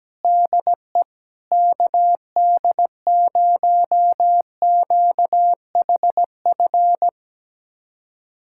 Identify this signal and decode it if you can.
Morse code, heard as a keyed tone, about 17 wpm, 700 Hz. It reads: DE KD0QHF
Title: USA Callsign DE Callsign - 6 Characters - 17wpm